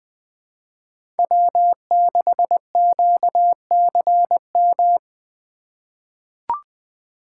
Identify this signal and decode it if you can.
Morse code, heard as a keyed tone, about 20 wpm, 700 Hz. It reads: W6QCM
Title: USA Call Signs - 5 Characters - 4x Character Spacing - 20wpm